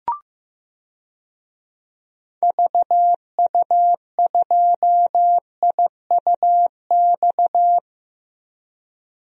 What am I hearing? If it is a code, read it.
Morse code, heard as a keyed tone, about 15 wpm, 700 Hz. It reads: VU2IUX
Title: DXCC-100 Call Signs - 6 Characters - Repeated 2x - 15wpm